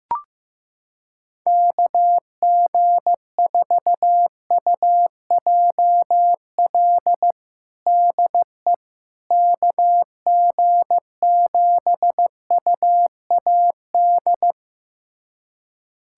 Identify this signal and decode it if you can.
Morse code, heard as a keyed tone, about 15 wpm, 700 Hz. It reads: KG4UJL DE KG7UAD
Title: USA Callsign DE Callsign 15wpm